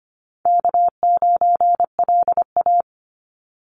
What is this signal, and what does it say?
Morse code, heard as a keyed tone, about 25 wpm, 700 Hz. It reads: K9LA
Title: Easy Call Signs Repeated 3x - 25wpm